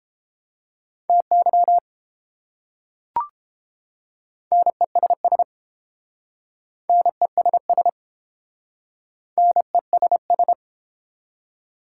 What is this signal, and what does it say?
Morse code, heard as a keyed tone, about 30 wpm, 700 Hz. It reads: TY NESS NESS NESS